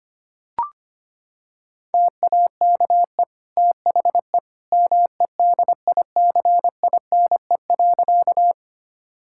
Morse code, heard as a keyed tone, about 25 wpm, 700 Hz. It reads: TAKE THE MEDICINE.
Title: Sentences from Top 1000 Words - Review Part 6 - 25wpm